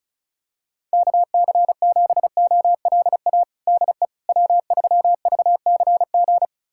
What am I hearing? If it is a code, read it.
Morse code, heard as a keyed tone, about 34 wpm, 700 Hz. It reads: KC7OLA DE W3VCG
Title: USA Callsign DE Callsign 35wpm